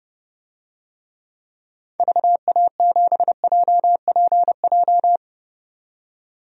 Morse code, heard as a keyed tone, about 30 wpm, 700 Hz. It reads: VA7JPJ
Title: DXCC-100 Call Signs - 6 Characters - 30wpm